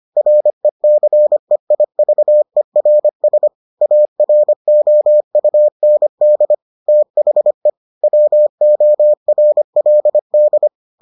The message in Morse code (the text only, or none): RECEIVERS AROUND THE WORLD